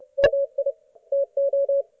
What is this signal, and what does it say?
Morse code, heard as a keyed tone, about 31 wpm, 560 Hz. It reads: MI TO
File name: Archiv / CUC179 / ok1fhi cul179